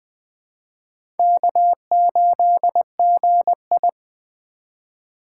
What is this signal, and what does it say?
Morse code, heard as a keyed tone, about 20 wpm, 700 Hz. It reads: K8GI
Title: USA Call Signs - 4 Characters - 3x Character Spacing - 20wpm